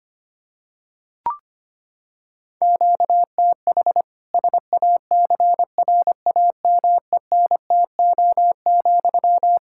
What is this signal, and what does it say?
Morse code, heard as a keyed tone, about 25 wpm, 700 Hz. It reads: QTH SACRAMENTO,
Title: Practice QSO - Exchange 1 - Beginning - 25wpm